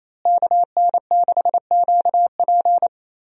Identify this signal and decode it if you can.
Morse code, heard as a keyed tone, about 27 wpm, 700 Hz. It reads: KN6QP
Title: USA Call Signs - 5 Characters - Repeated 3x - 28wpm